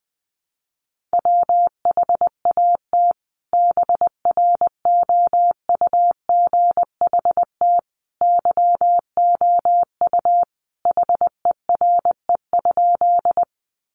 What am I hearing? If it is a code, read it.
Morse code, heard as a keyed tone, about 20 wpm, 700 Hz. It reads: WHAT BROUGHT YOU HERE?
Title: Sentences from Top 500 Words 20wpm